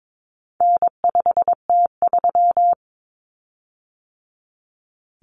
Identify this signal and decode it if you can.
Morse code, heard as a keyed tone, about 22 wpm, 700 Hz. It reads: N5T3
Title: Common Cut Numbers - 4-Digit 22wpm